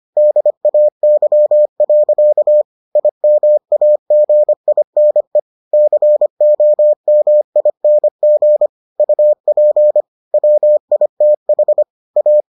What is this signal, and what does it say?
Morse code, heard as a keyed tone, about 25 wpm, 600 Hz. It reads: DAY. IMAGINE COMING UP WITH A